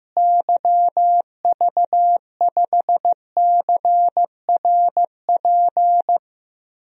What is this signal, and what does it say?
Morse code, heard as a keyed tone, about 15 wpm, 700 Hz. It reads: YV5CRP
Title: DXCC-100 Call Signs - 6 Characters - Repeated 3x - 15wpm